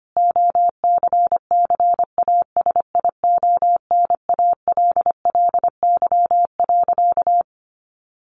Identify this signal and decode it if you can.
Morse code, heard as a keyed tone, about 25 wpm, 700 Hz. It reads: OCCASIONALLY.
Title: Sentences from Top 1300 Words 25wpm